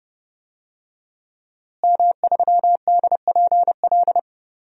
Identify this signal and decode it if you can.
Morse code, heard as a keyed tone, about 31 wpm, 700 Hz. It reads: M3DPL
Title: DXCC-100 Call Signs - 5 Characters - Repeated 3x - 30wpm